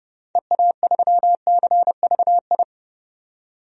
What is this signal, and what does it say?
Morse code, heard as a keyed tone, about 30 wpm, 700 Hz. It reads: EA3CVI
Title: DXCC-100 Call Signs - 6 Characters - 30wpm